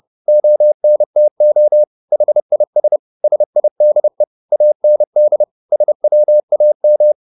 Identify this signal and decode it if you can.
Morse code, heard as a keyed tone, about 30 wpm, 600 Hz. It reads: ONTO HIS SIDE AND SWAM